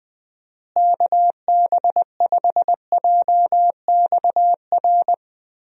Morse code, heard as a keyed tone, about 20 wpm, 700 Hz. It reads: KB5JXR
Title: USA Call Signs - 6 Characters - Repeated 3x - 20wpm